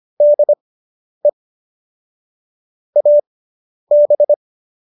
Morse code, heard as a keyed tone, about 25 wpm, spaced as for 10 wpm, 590 Hz. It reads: DE AB